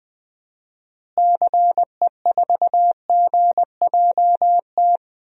Morse code, heard as a keyed tone, about 20 wpm, 700 Hz. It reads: CE4GJT
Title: DXCC-100 Call Signs - 6 Characters - Repeated 3x - 20wpm